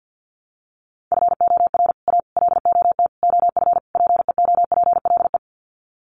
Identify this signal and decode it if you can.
Morse code, heard as a keyed tone, about 50 wpm, 700 Hz. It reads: FOR A LOT OF PEOPLE
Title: Sets of 5 Words Encore 50wpm